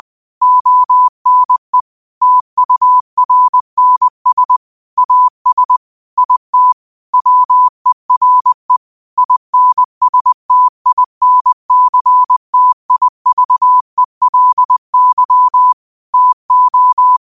Morse code, heard as a keyed tone, about 20 wpm, 1000 Hz. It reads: ONE TURNS AS IT WERE INSTINCTIVELY TO